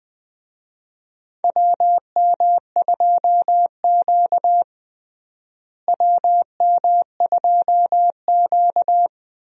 Morse code, heard as a keyed tone, about 20 wpm, 700 Hz. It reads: WM2Q WM2Q
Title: USA Call Signs - 4 Characters - Repeated 2x - 20wpm